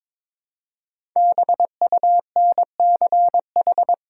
Morse code, heard as a keyed tone, about 22 wpm, 700 Hz. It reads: BUNCH
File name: 5 Letter Words - First 3 Letters Given - 22wpm